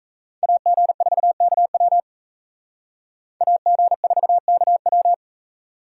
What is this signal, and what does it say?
Morse code, heard as a keyed tone, about 36 wpm, 700 Hz. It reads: AG4KW AG4KW